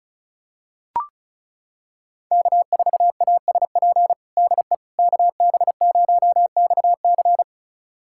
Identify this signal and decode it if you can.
Morse code, heard as a keyed tone, about 36 wpm, 700 Hz. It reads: K4ASP DE KB0XC
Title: USA Callsign DE Callsign - 5 Characters - 35wpm